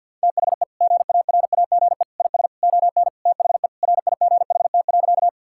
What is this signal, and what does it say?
Morse code, heard as a keyed tone, about 50 wpm, 700 Hz. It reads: THE GARAGE IS ON THE RIGHT.
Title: Sentences from Top 1600 Words 50wpm